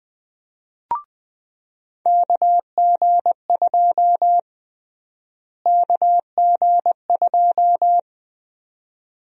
Morse code, heard as a keyed tone, about 20 wpm, 700 Hz. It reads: KG2 KG2
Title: USA Call Sign Prefixes - Repeated 2x - 20wpm